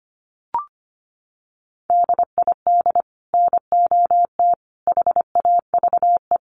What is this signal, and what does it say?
Morse code, heard as a keyed tone, about 25 wpm, 700 Hz. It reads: DID NOT HAVE
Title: Sets of 3 Words 25wpm